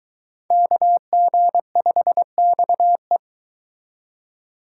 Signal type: Morse code

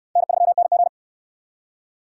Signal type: Morse code

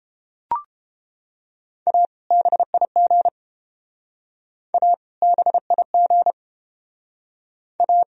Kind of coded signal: Morse code

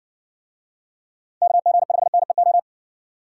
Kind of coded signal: Morse code

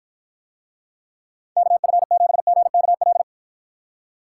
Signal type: Morse code

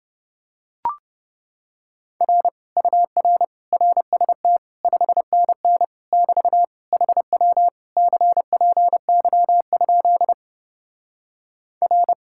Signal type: Morse code

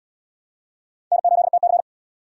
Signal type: Morse code

